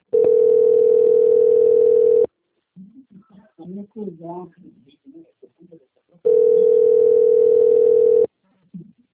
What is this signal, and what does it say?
Ringback tone (North America), 440 + 480 Hz, with speech.
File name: out- - - - - .64815